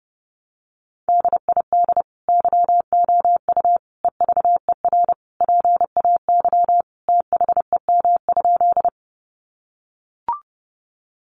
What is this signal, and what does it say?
Morse code, heard as a keyed tone, about 30 wpm, 700 Hz. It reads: DID YOU EVER PAY THEM?
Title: Sentences from Top 700 Words 30wpm